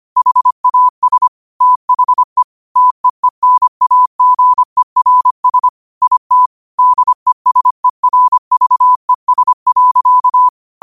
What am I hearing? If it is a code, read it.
Morse code, heard as a keyed tone, about 25 wpm, 1000 Hz. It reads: HAS THE TEENAGERS IT DESERVES.
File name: qotdfiles / 25wpm